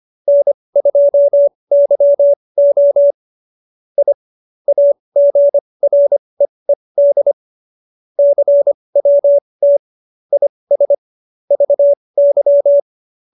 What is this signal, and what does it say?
Morse code, heard as a keyed tone, about 25 wpm, spaced as for 20 wpm, 570 Hz. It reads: N2YO I AGREED CWT IS VY
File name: QSO316 20